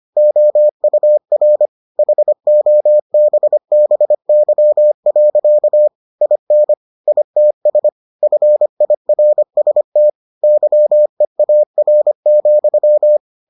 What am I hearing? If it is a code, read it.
Morse code, heard as a keyed tone, about 25 wpm, 600 Hz. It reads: OUR HOBBY. IN ITS FIRST YEAR,